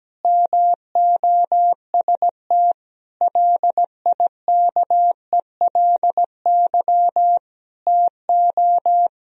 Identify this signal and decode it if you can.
Morse code, heard as a keyed tone, about 17 wpm, 700 Hz. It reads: MOST LIKELY TO
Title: Sentences from Top 1000 Words - Review Part 4 - 17wpm